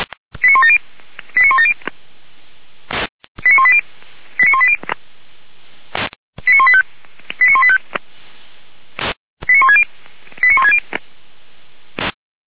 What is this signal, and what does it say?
Five-tone selective call (ZVEI), heard as quick runs of five tones, tones of about 70 ms. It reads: 89170 89170 89180 89180 89185 89185 89160 89160